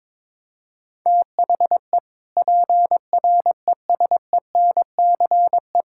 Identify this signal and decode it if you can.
Morse code, heard as a keyed tone, about 22 wpm, 700 Hz. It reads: THE PRESENCE
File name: practice / 22wpm / Sentences with 3-Word Phrases - 22wpm